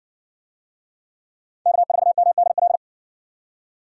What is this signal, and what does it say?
Morse code, heard as a keyed tone, about 56 wpm, 700 Hz. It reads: K4MDL